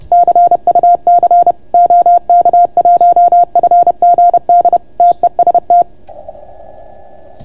Morse code, heard as a keyed tone, about 30 wpm, 690 Hz. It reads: CUC OK1FGD TEST